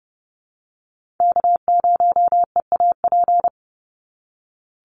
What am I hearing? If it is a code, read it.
Morse code, heard as a keyed tone, about 32 wpm, 700 Hz. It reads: K0EAP
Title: NA SOTA Chaser Callsigns 30wpm